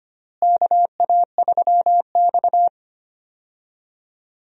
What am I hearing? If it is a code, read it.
Morse code, heard as a keyed tone, about 25 wpm, 700 Hz. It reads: KA3X
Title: USA Call Signs - 4 Characters - Repeated 3x - 25wpm